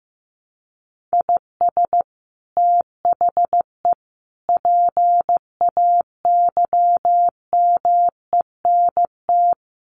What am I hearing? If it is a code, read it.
Morse code, heard as a keyed tone, about 15 wpm, 700 Hz. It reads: IS THE PAYMENT
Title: Sentences from Top 1500 Words 15wpm